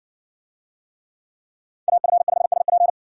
Morse code, heard as a keyed tone, about 57 wpm, 700 Hz. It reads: AF5SP